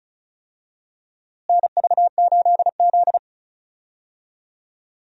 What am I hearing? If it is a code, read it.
Morse code, heard as a keyed tone, about 36 wpm, 700 Hz. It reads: NV8Z
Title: USA Call Signs - 4 Characters - Repeated 3x - 35wpm